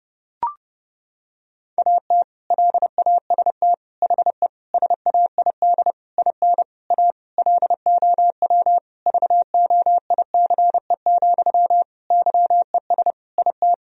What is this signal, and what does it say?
Morse code, heard as a keyed tone, about 30 wpm, 700 Hz. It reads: AT LAST HE SAID IN A LOW VOICE, YES IT
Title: Sentences from Top 1000 Words - Review Part 1 - 30wpm